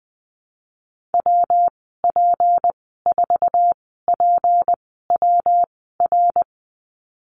Morse code, heard as a keyed tone, about 20 wpm, spaced as for 14 wpm, 700 Hz. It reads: WP4PWR
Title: 213 Lesson 34g - Number 8 - Call Signs 20wpm